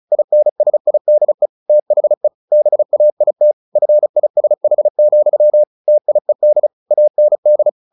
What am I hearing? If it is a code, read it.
Morse code, heard as a keyed tone, about 34 wpm, 600 Hz. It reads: INSIDE THE BAIT FISH, TIED AND
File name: the old man and the sea / the old man and the sea 1 35wpm